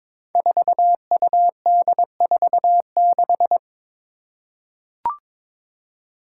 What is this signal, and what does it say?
Morse code, heard as a keyed tone, about 22 wpm, 700 Hz. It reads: 4UD46
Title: Five Challenging Letter-Number Combinations 22wpm